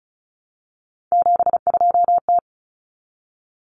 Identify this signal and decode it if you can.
Morse code, heard as a keyed tone, about 34 wpm, 700 Hz. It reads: 72T
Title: Common Cut Numbers - 3-Digit 35wpm